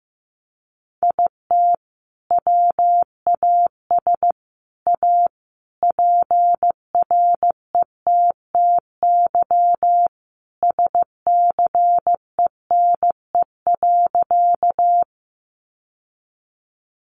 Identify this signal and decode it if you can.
Morse code, heard as a keyed tone, about 15 wpm, 700 Hz. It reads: IT WAS A PRETTY SCENE.
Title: Sentences from Top 1100 Words 15wpm